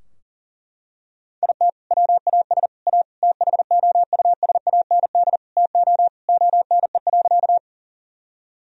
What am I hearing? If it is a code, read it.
Morse code, heard as a keyed tone, about 40 wpm, 700 Hz. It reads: IT WAS A THOUSAND TO ONE.
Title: Sentences from Top 1000 Words - Review Part 3 - 40wpm